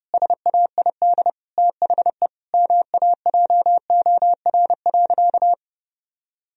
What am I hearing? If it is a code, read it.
Morse code, heard as a keyed tone, about 32 wpm, 700 Hz. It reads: SAID THE MAJOR.
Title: Sentences from Top 1000 Words - Review Part 4 - 30wpm